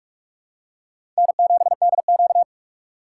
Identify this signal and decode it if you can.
Morse code, heard as a keyed tone, about 43 wpm, 700 Hz. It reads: N7DQ
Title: USA Call Signs - 4 Characters - 6x Character Spacing - 45wpm